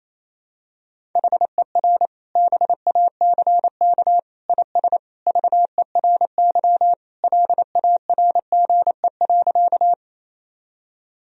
Morse code, heard as a keyed tone, about 27 wpm, 700 Hz. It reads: HER BACK IS VERY LARGE.
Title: Sentences from Top 1000 Words - Review Part 1 - 28wpm